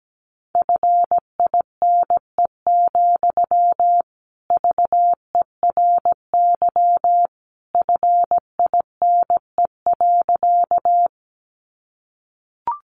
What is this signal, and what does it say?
Morse code, heard as a keyed tone, about 17 wpm, 700 Hz. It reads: FINE, VERY FINE.
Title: Sentences from Top 500 Words 17wpm